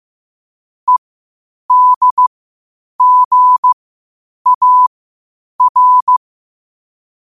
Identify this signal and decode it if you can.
Morse code, heard as a keyed tone, about 15 wpm, spaced as for 8 wpm, 1000 Hz. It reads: EDGAR
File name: qotdfiles / 8wpm